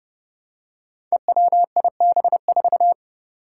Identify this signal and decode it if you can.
Morse code, heard as a keyed tone, about 30 wpm, 700 Hz. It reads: EWIB4
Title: Five Challenging Letter-Number Combinations 30wpm